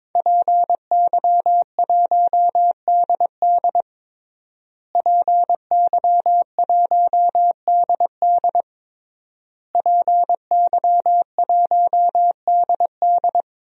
Morse code, heard as a keyed tone, about 23 wpm, 700 Hz. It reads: PY1DD PY1DD PY1DD
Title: DXCC-100 Call Signs - 5 Characters - Repeated 3x - 22wpm